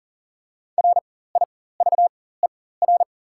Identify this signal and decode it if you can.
Morse code, heard as a keyed tone, about 40 wpm, spaced as for 19 wpm, 700 Hz. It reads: RIVER
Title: 4x Character Spacing - Top 300 Words - 40wpm